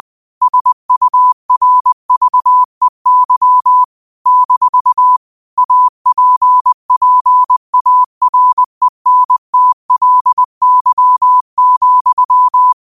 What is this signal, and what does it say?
Morse code, heard as a keyed tone, about 20 wpm, 1000 Hz. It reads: SURVEY - APPARENTLY,